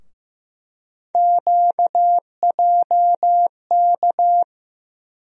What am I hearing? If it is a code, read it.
Morse code, heard as a keyed tone, about 15 wpm, 700 Hz. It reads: QJK
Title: USA Call Sign Postfixes - 3 Letters 15wpm